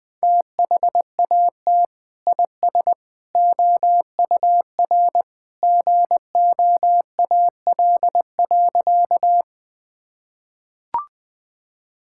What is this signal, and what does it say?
Morse code, heard as a keyed tone, about 20 wpm, 700 Hz. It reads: THAT IS OUR GOAL.
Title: Sentences from Top 1000 Words - Review Part 5 - 20wpm